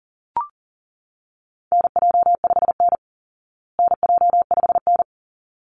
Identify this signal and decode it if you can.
Morse code, heard as a keyed tone, about 40 wpm, 700 Hz. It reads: NJ5N NJ5N
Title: Call Signs Repeated 3x - 40wpm